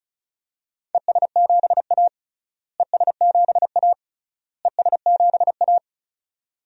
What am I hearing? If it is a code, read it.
Morse code, heard as a keyed tone, about 34 wpm, 700 Hz. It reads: ES7A ES7A ES7A
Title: DXCC-100 Call Signs - 4 Characters - Repeated 3x - 35wpm